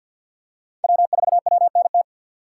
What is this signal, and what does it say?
Morse code, heard as a keyed tone, about 50 wpm, 700 Hz. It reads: W4WNT
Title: K1USN SST QSO Practice - 50wpm